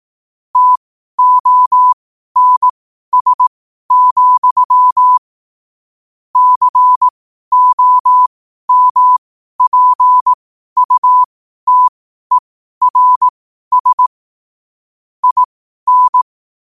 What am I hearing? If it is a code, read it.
Morse code, heard as a keyed tone, about 18 wpm, spaced as for 12 wpm, 1000 Hz. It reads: TONS, COMPUTERS IN
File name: qotdfiles / 12wpm